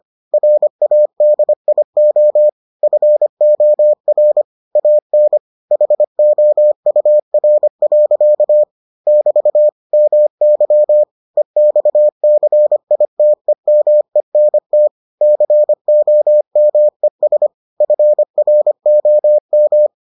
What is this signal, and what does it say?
Morse code, heard as a keyed tone, about 25 wpm, 600 Hz. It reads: RADIO FOR AN HOUR. <BT> MY EXCITEMENT COMES FROM